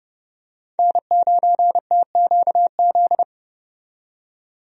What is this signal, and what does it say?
Morse code, heard as a keyed tone, about 30 wpm, 700 Hz. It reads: N9TQZ
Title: USA Call Signs - 5 Characters - 30wpm